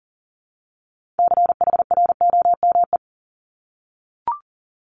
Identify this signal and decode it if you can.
Morse code, heard as a keyed tone, about 40 wpm, 690 Hz. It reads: CHROME
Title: Multi-letter Phonemes Repeated 3x 40wpm